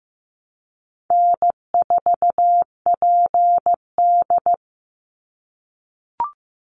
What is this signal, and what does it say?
Morse code, heard as a keyed tone, about 15 wpm, 700 Hz. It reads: N4PD
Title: Easy Call Signs - 15wpm